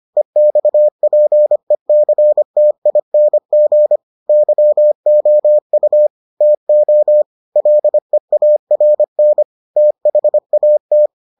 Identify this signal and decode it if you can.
Morse code, heard as a keyed tone, about 25 wpm, 600 Hz. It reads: EXPECTING YOU TO LEARN THAT